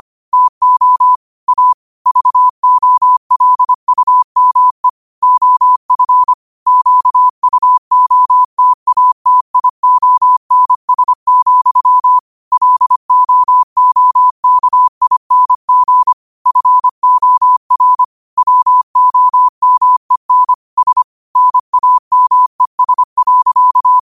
Morse code, heard as a keyed tone, about 25 wpm, 1000 Hz. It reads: TO A VOLUME OF QUOTATIONS, LOOKING FOR WOMEN S NAMES.
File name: qotdfiles / 25wpm